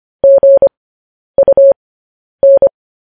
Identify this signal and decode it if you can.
Morse code, heard as a keyed tone, about 25 wpm, spaced as for 10 wpm, 550 Hz. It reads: GUN